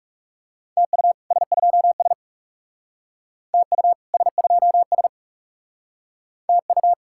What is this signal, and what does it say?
Morse code, heard as a keyed tone, about 38 wpm, 700 Hz. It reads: TU S2S TU S2S TU